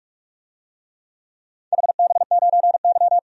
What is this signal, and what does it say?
Morse code, heard as a keyed tone, about 46 wpm, 700 Hz. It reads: HB9Y
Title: DXCC-100 Call Signs - 4 Characters - 45wpm